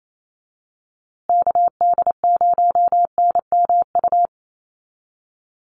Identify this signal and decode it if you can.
Morse code, heard as a keyed tone, about 28 wpm, 700 Hz. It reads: KD0NMU
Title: USA Call Signs Repeated 2x 28wpm